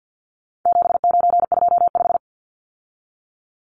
Morse code, heard as a keyed tone, about 50 wpm, 700 Hz. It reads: 7925
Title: Numbers - 4-Digit 50wpm